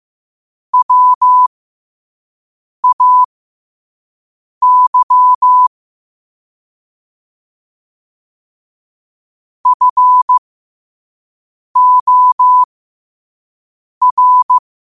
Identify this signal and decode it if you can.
Morse code, heard as a keyed tone, about 15 wpm, spaced as for 5 wpm, 1000 Hz. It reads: WAY FOR
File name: qotdfiles / 5wpm